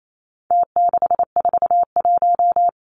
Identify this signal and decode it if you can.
Morse code, heard as a keyed tone, about 28 wpm, 700 Hz. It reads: T641
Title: Numbers - Contest Style - 4-Digit 28wpm